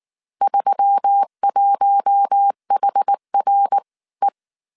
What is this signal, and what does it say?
Morse code, heard as a keyed tone, about 19 wpm, spaced as for 17 wpm, 800 Hz. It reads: 31HRE